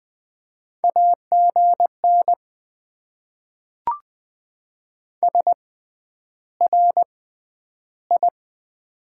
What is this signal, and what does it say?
Morse code, heard as a keyed tone, about 20 wpm, spaced as for 10 wpm, 700 Hz. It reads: AGN SRI
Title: Top 90 QSO Elements - 6x Character Spacing - 20wpm